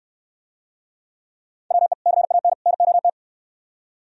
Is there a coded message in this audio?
Yes: Morse code from 1.7 to 3.1 s, heard as a keyed tone, about 66 wpm, 700 Hz. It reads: WE CAN NOT